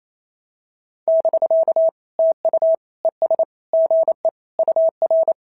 Morse code, heard as a keyed tone, about 27 wpm, 650 Hz. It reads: <BK> TU ES GE UR
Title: SOTA QSO Practice 28wpm